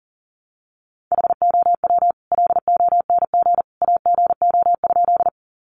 Morse code, heard as a keyed tone, about 40 wpm, 700 Hz. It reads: HOW LONG AGO?
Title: Sentences from Top 500 Words 40wpm